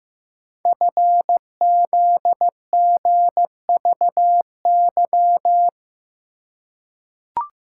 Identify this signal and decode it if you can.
Morse code, heard as a keyed tone, about 15 wpm, 700 Hz. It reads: FZGVY
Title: Code Groups - 5 Letters - 15wpm